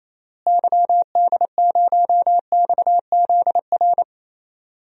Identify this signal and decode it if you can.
Morse code, heard as a keyed tone, about 28 wpm, 700 Hz. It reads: YD0XZR
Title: DXCC-100 Call Signs - 6 Characters - Repeated 2x - 28wpm